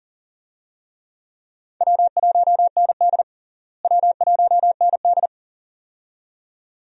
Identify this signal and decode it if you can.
Morse code, heard as a keyed tone, about 40 wpm, 700 Hz. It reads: W1ND W1ND
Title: USA Call Signs - 4 Characters - Repeated 2x - 40wpm